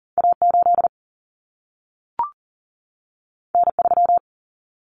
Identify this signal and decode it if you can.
Morse code, heard as a keyed tone, about 40 wpm, 700 Hz. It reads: A8 N3
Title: Common Cut Numbers - 2-Digit 40wpm